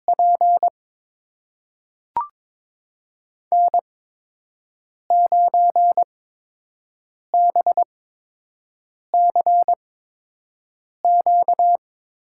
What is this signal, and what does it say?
Morse code, heard as a keyed tone, about 22 wpm, spaced as for 6 wpm, 700 Hz. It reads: P N9BCQ
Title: USA Call Signs - 5 Characters - 8x Character Spacing - 22wpm